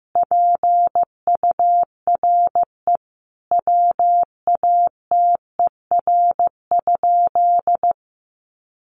Morse code, heard as a keyed tone, about 15 wpm, 700 Hz. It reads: PURE WATER?
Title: Sentences from Top 1500 Words 15wpm